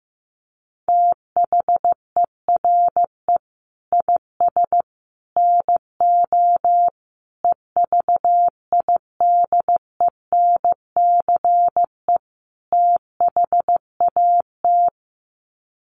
Morse code, heard as a keyed tone, about 15 wpm, 700 Hz. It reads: THERE IS NO EVIDENCE THAT
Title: Sets of 5 Words Encore 15wpm